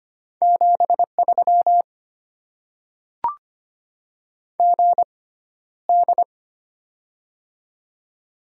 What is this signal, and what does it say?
Morse code, heard as a keyed tone, about 25 wpm, spaced as for 13 wpm, 690 Hz. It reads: 73 GD
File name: Top 90 QSO Elements - 6x Character Spacing - 25wpm